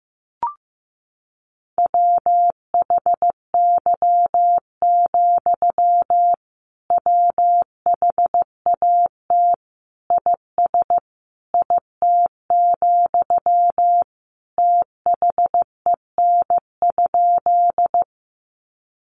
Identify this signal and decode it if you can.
Morse code, heard as a keyed tone, about 15 wpm, 700 Hz. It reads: WHY, WHAT IS IT, THEN?
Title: Sentences from Top 200 Words 15wpm